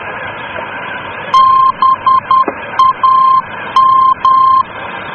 Morse code, heard as a keyed tone, about 10 wpm, 1050 Hz. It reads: BAM
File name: germany / mx / BAM-D